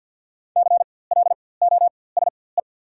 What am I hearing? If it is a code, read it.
Morse code, heard as a keyed tone, about 54 wpm, spaced as for 24 wpm, 700 Hz. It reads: CLOSE
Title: 4x Character Spacing - Top 500 Words - 50wpm